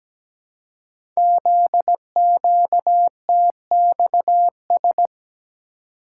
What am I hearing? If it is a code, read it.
Morse code, heard as a keyed tone, about 17 wpm, 700 Hz. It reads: ZQTXS
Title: Code Groups - 5 Letters - 17wpm